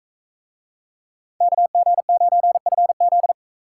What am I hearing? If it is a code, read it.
Morse code, heard as a keyed tone, about 39 wpm, 700 Hz. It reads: KG9FZ